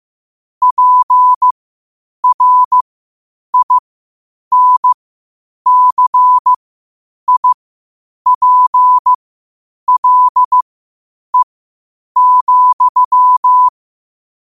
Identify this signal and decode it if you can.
Morse code, heard as a keyed tone, about 15 wpm, spaced as for 8 wpm, 1000 Hz. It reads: PRINCIPLE,
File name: qotdfiles / 8wpm